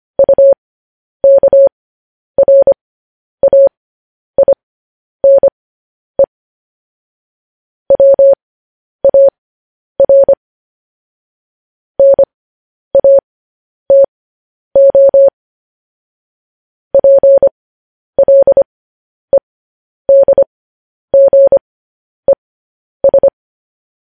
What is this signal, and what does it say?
Morse code, heard as a keyed tone, about 25 wpm, spaced as for 10 wpm, 550 Hz. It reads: UKRAINE WAR NATO PLEDGES
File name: podcast / News headlines-1x-25x10